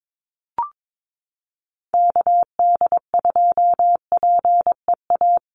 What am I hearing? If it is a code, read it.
Morse code, heard as a keyed tone, about 22 wpm, 700 Hz. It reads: KD2PEA
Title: USA Call Signs 22wpm